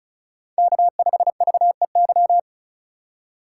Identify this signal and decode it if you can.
Morse code, heard as a keyed tone, about 36 wpm, 700 Hz. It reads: K5VEY
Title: USA Call Signs - 5 Characters - 3x Character Spacing - 35wpm